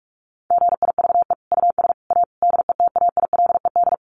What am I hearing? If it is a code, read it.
Morse code, heard as a keyed tone, about 44 wpm, 700 Hz. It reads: GIVE US A DETAILED
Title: Sentences from Top 1500 Words 45wpm